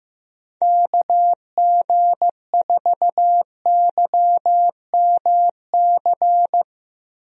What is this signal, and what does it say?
Morse code, heard as a keyed tone, about 15 wpm, 700 Hz. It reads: KG4YMC
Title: USA Call Signs - 6 Characters - Repeated 2x - 15wpm